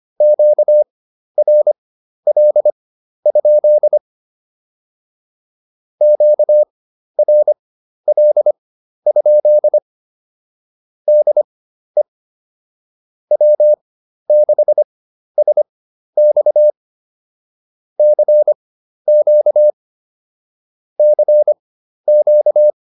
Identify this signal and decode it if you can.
Morse code, heard as a keyed tone, about 25 wpm, spaced as for 12 wpm, 600 Hz. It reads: QRL? QRL? DE W6SX CQ CQ